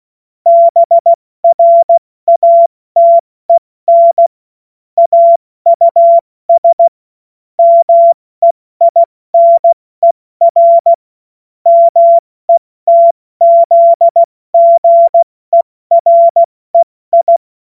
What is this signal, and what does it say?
Morse code, heard as a keyed tone, about 16 wpm, 680 Hz. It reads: BRATEN AUS MEINER METZGEREI
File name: Morsekurs / Witze / Witz 04. Oktober 2022 70 F80